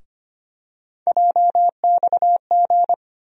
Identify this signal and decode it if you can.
Morse code, heard as a keyed tone, about 25 wpm, 700 Hz. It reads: JXG
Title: USA Call Sign Postfixes - 3 Letters 25wpm